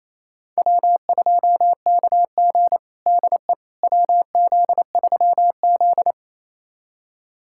Morse code, heard as a keyed tone, about 28 wpm, 700 Hz. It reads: W2KG DE WZ3Z
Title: USA Callsign DE Callsign - 4 Characters - 28wpm